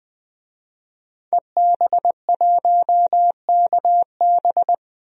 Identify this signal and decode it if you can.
Morse code, heard as a keyed tone, about 20 wpm, 700 Hz. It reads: EB1KB
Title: DXCC-100 Call Signs - 5 Characters - Repeated 3x - 20wpm